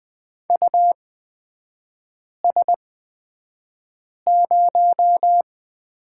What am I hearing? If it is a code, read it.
Morse code, heard as a keyed tone, about 20 wpm, spaced as for 5 wpm, 690 Hz. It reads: US0
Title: 229 Lesson 37e - Letter x - Three Characters 20wpm